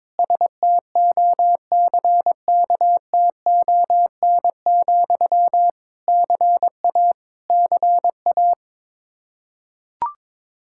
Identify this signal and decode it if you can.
Morse code, heard as a keyed tone, about 22 wpm, 700 Hz. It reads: STOCKTON, CA CA
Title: Practice QSO - Exchange 1 - Beginning - 22wpm